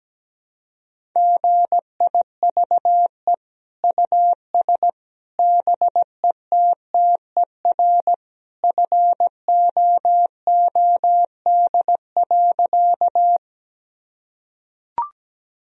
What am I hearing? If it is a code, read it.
Morse code, heard as a keyed tone, about 17 wpm, 700 Hz. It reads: GIVE US BETTER FOOD.
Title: Sentences from Top 1000 Words - Review Part 1 - 17wpm